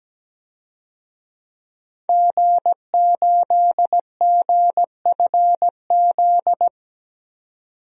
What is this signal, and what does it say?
Morse code, heard as a keyed tone, about 17 wpm, 700 Hz. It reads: G8GFZ
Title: DXCC-100 Call Signs - 5 Characters - 17wpm